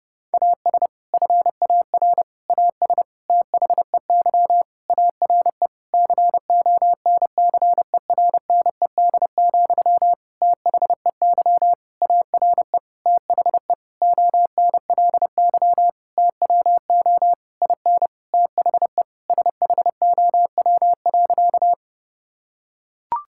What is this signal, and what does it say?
Morse code, heard as a keyed tone, about 30 wpm, 700 Hz. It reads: AS FAR AS THEY ARE CONCERNED, THEY ARE THE ONLY TWO IN THE SHOW.
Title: Sentences from Top 1000 Words - Review Part 6 - 30wpm